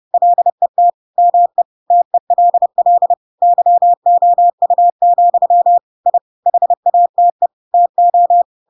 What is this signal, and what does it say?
Morse code, heard as a keyed tone, about 30 wpm, 700 Hz. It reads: LET ME TELL YOU, I HATE TO
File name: Pinocchio / Pinocchio ch4 30 30 ik1hge